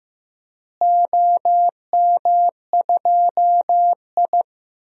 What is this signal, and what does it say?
Morse code, heard as a keyed tone, about 15 wpm, 700 Hz. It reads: OM2I
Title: DXCC-100 Call Signs - 4 Characters - Repeated 3x - 15wpm